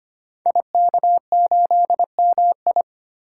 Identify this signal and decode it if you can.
Morse code, heard as a keyed tone, about 25 wpm, 700 Hz. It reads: IK8MI
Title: Code Groups - 5 Letter:Numbers - 25wpm